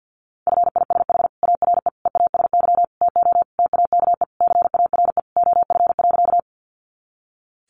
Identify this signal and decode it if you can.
Morse code, heard as a keyed tone, about 48 wpm, 700 Hz. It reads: FISH ARE EASY TO TAKE CARE OF.
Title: Additional Easy Phrases 50wpm